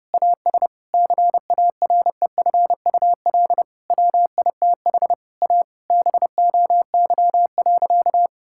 Morse code, heard as a keyed tone, about 30 wpm, 700 Hz. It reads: AS CAREFUL WITH A BOY.
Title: Sentences from Top 1600 Words 30wpm